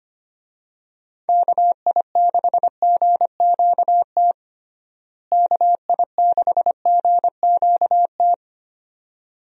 Morse code, heard as a keyed tone, about 25 wpm, 700 Hz. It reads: KI6GQT KI6GQT
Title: USA Call Signs - 6 Characters - Repeated 2x - 25wpm